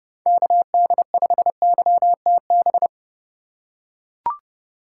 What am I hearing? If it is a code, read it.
Morse code, heard as a keyed tone, about 29 wpm, 700 Hz. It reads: KD5YTB